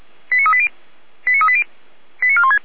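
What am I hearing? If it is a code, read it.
Five-tone selective call (ZVEI), heard as quick runs of five tones, tones of about 80 ms. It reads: 89380 89380 89416
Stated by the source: ZVEI five-tone selective call received with an SDR, archived by the Feuerwehr Orlamünde fire brigade.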